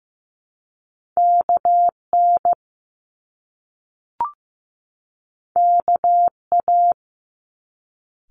Morse code, heard as a keyed tone, about 15 wpm, 700 Hz. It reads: KN KA
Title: Top Two Letter Combinations in Words 15wpm